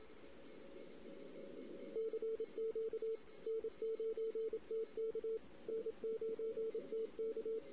Morse code, heard as a keyed tone, about 26 wpm, 450 Hz. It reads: CQ N9TK N9TK
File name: CW / 2008-ss-cw-k5tr-op-n5rz -16:00